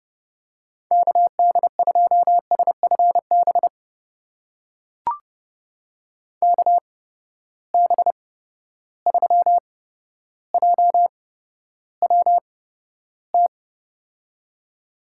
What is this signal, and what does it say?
Morse code, heard as a keyed tone, about 30 wpm, spaced as for 13 wpm, 700 Hz. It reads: KD2SFB KB3JWT
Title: USA Call Signs - 6 Characters - 8x Character Spacing - 30wpm